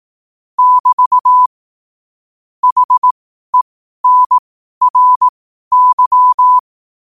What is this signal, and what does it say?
Morse code, heard as a keyed tone, about 18 wpm, spaced as for 12 wpm, 1000 Hz. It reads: <BT> HENRY
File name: qotdfiles / 12wpm